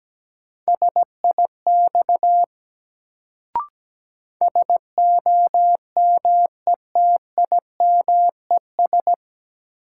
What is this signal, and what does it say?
Morse code, heard as a keyed tone, about 17 wpm, 700 Hz. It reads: SIX SOMETIMES
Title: Sentences from Top 600 Words - New Word Drill - 17wpm